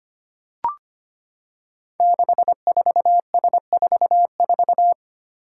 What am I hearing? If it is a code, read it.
Morse code, heard as a keyed tone, about 25 wpm, 700 Hz. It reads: 64S44
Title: Five Challenging Letter-Number Combinations 25wpm